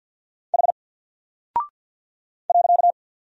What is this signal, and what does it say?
Morse code, heard as a keyed tone, about 50 wpm, 700 Hz. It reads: H .
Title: Single Character (Letter:Number:Punctuation:Prosign) Alt 50wpm